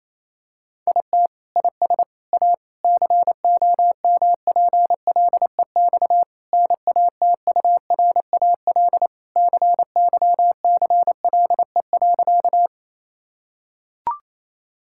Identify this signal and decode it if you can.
Morse code, heard as a keyed tone, about 28 wpm, 700 Hz. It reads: IT IS A COMPLEX NATURAL CYCLE.
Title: Sentences from Top 1000 Words - Review Part 6 - 28wpm